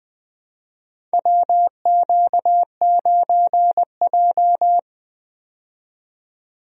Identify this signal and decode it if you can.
Morse code, heard as a keyed tone, about 20 wpm, 700 Hz. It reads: WQ9J
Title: USA Call Signs - 4 Characters - 20wpm